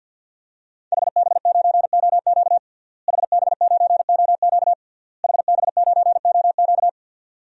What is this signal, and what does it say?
Morse code, heard as a keyed tone, about 52 wpm, 700 Hz. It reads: HB9OQ HB9OQ HB9OQ
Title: DXCC-100 Call Signs - 5 Characters - Repeated 3x - 50wpm